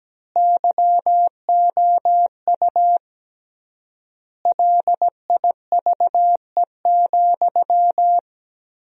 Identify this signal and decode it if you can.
Morse code, heard as a keyed tone, about 17 wpm, 700 Hz. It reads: YOU LIVE,
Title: 3x Word-Spacing - Sentences from Top 200 Words - 17wpm